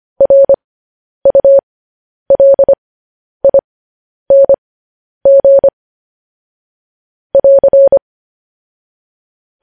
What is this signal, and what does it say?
Morse code, heard as a keyed tone, about 25 wpm, spaced as for 10 wpm, 550 Hz. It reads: RULING <AR>
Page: news headlines in morse code at 10 wpm